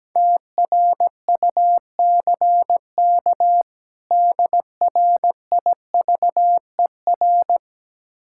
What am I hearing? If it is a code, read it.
Morse code, heard as a keyed tone, about 17 wpm, 700 Hz. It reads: TRUCK DRIVER
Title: Practice QSO - Exchange 3 - More Info - 17wpm